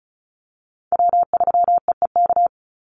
Morse code, heard as a keyed tone, about 34 wpm, 700 Hz. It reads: W3EEK
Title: Call Signs Repeated 3x - 35wpm